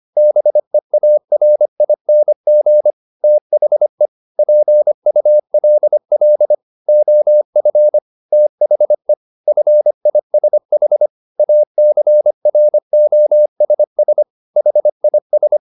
Morse code, heard as a keyed tone, about 25 wpm, 600 Hz. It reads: BEARING THE PULL OF THE FISH ACROSS HIS